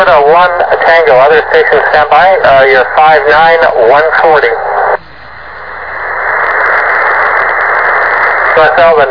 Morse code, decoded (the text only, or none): none